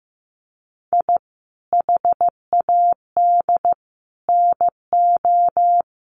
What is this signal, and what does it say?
Morse code, heard as a keyed tone, about 15 wpm, 700 Hz. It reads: I HAD NO
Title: Sets of 5 Words Encore 15wpm